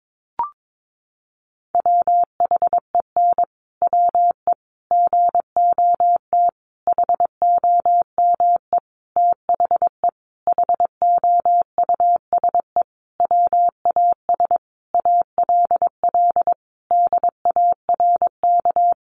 Morse code, heard as a keyed tone, about 22 wpm, 700 Hz. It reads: WHEN WE GOT HOME THE HOUSE WAS ALL DARK
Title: Sentences from Top 500 Words 22wpm